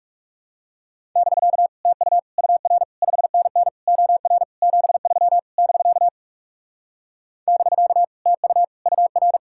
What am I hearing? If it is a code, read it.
Morse code, heard as a keyed tone, about 40 wpm, 700 Hz. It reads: <BK> TU UR 5NN OR 73 <BK> <BK> TU UR